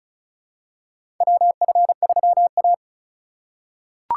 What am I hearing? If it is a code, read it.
Morse code, heard as a keyed tone, about 36 wpm, 700 Hz. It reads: WF3A
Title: USA Call Signs - 4 Characters - 4x Character Spacing - 35wpm